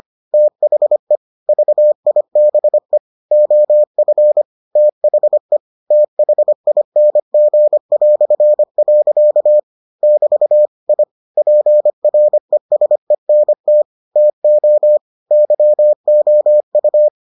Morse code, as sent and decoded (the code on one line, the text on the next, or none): - .... / ...- .. -... / --- ..-. / - .... / - .... .. -. --. .-..-. .-.-.- / -...- / .. / .--. .-. ... -. - / - --- / -.-- --- ..-
THE VIBE OF THE THING". <BT> I PRESENT TO YOU